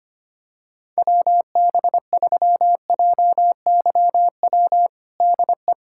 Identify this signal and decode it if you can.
Morse code, heard as a keyed tone, about 25 wpm, 700 Hz. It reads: WB3JYW DE
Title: USA Callsign DE Callsign - 6 Characters - 25wpm